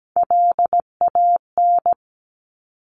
Morse code, heard as a keyed tone, about 17 wpm, 700 Hz. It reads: LAN
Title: Top Three Letter Combinations in Words 17wpm